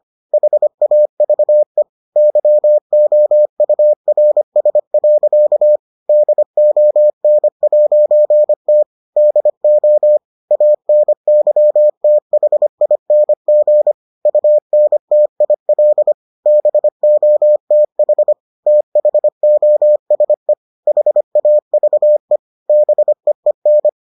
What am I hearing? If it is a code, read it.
Morse code, heard as a keyed tone, about 25 wpm, 600 Hz. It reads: HAVE YOURS. DON'T DO ANYTHING UNTIL BOTH THOSE HAVE BEEN